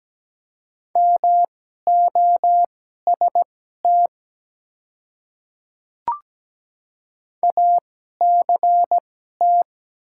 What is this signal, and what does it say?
Morse code, heard as a keyed tone, about 17 wpm, spaced as for 12 wpm, 700 Hz. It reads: MOST ACT
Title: 2x Character Spacing - Top 200 Words - 17wpm